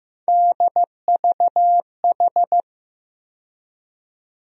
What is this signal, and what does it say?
Morse code, heard as a keyed tone, about 15 wpm, 700 Hz. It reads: DVH
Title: USA Call Sign Postfixes - 3 Letters 15wpm